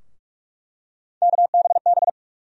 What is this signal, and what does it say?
Morse code, heard as a keyed tone, about 44 wpm, 700 Hz. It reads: KBB
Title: USA Call Sign Postfixes - 3 Letters 45wpm